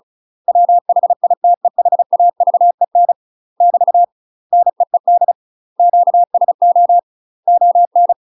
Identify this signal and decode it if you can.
Morse code, heard as a keyed tone, about 35 wpm, 700 Hz. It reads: WHITEHAVEN <BT> NEED QSO ON
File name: UK MASTER QSOs TRAINING / 35WPM / UK MASTER QSOs 0022